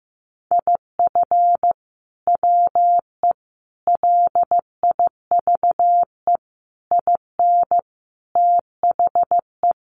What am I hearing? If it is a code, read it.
Morse code, heard as a keyed tone, about 15 wpm, 700 Hz. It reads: IF WE LIVE IN THE
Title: Sentences from Top 1400 Words 15wpm